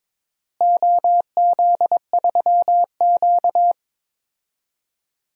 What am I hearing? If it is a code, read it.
Morse code, heard as a keyed tone, about 22 wpm, 700 Hz. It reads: OZ3Q
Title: DXCC-100 Call Signs - 4 Characters - Repeated 3x - 22wpm